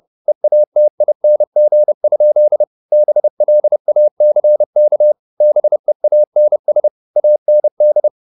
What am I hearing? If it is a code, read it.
Morse code, heard as a keyed tone, about 30 wpm, 600 Hz. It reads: EATING? BLACK BEANS AND
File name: the old man and the sea / the old man and the sea 1 30wpm